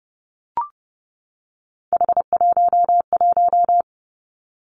Morse code, heard as a keyed tone, about 30 wpm, 700 Hz. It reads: H11
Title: Three Challenging Letter-Number Combinations 30wpm